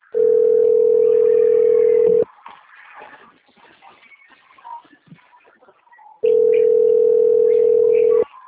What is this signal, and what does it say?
Ringback tone (North America), 440 + 480 Hz.
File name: rg-1600-unknown- - - .36006